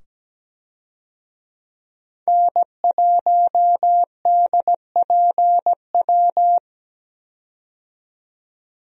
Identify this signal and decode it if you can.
Morse code, heard as a keyed tone, about 17 wpm, 700 Hz. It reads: N1DPW
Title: USA Call Signs - 5 Characters - 17wpm